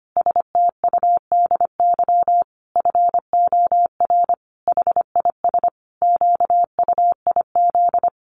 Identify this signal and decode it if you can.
Morse code, heard as a keyed tone, about 25 wpm, 700 Hz. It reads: STUDY FOR HIS QUIZ
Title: Additional Easy Phrases 25wpm